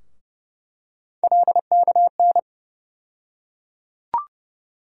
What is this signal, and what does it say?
Morse code, heard as a keyed tone, about 30 wpm, 700 Hz. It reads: LKN
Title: USA Call Sign Postfixes - 3 Letters 30wpm